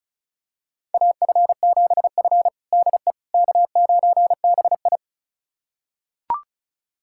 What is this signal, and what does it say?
Morse code, heard as a keyed tone, about 36 wpm, 700 Hz. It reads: AF7F DE K9BI
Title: USA Callsign DE Callsign - 4 Characters - 35wpm